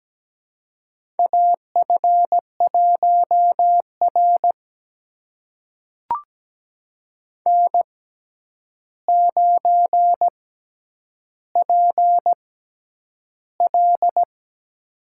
Morse code, heard as a keyed tone, about 17 wpm, spaced as for 9 wpm, 700 Hz. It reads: AF1R N9PL
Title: USA Call Signs - 4 Characters - 6x Character Spacing - 17wpm